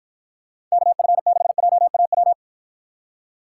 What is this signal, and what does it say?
Morse code, heard as a keyed tone, about 54 wpm, 700 Hz. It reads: KF6JAW